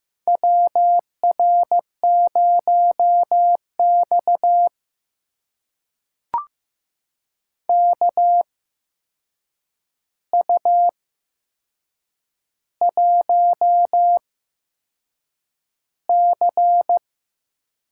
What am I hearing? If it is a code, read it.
Morse code, heard as a keyed tone, about 15 wpm, spaced as for 6 wpm, 700 Hz. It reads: WR0X KU1C
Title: USA Call Signs - 4 Characters - 8x Character Spacing - 15wpm